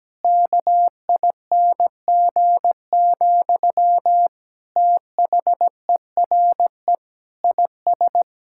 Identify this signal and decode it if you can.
Morse code, heard as a keyed tone, about 17 wpm, 700 Hz. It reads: KING, THERE IS
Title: Sentences from Top 1000 Words - Review Part 6 - 17wpm